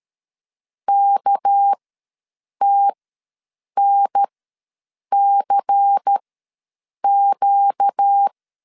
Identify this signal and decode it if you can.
Morse code, heard as a keyed tone, about 13 wpm, spaced as for 7 wpm, 800 Hz. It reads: KTNCQ
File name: LEZIONI / 6 PKX0 / 6 PK ws 12wpm 3116